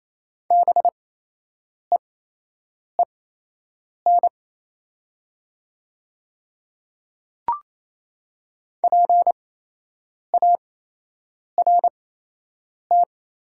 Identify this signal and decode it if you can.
Morse code, heard as a keyed tone, about 28 wpm, spaced as for 8 wpm, 700 Hz. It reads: BEEN PART
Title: 8x Character Spacing - Top 200 Words - 28wpm